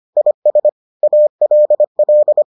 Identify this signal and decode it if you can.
Morse code, heard as a keyed tone, about 25 wpm, 600 Hz. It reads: IS ALL